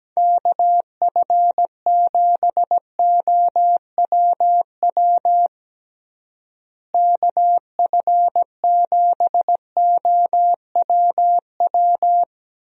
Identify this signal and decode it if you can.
Morse code, heard as a keyed tone, about 17 wpm, 700 Hz. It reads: KF7OWW KF7OWW
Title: USA Call Signs - 6 Characters - Repeated 3x - 17wpm